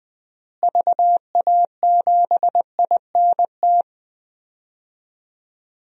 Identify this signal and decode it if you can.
Morse code, heard as a keyed tone, about 20 wpm, 700 Hz. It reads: VA7INT
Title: DXCC-100 Call Signs - 6 Characters - Repeated 3x - 20wpm